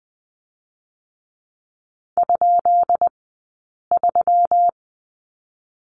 Morse code, heard as a keyed tone, about 19 wpm, 700 Hz. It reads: ? 3
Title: 132 Lesson 22d - Question Mark - Two Characters 20wpm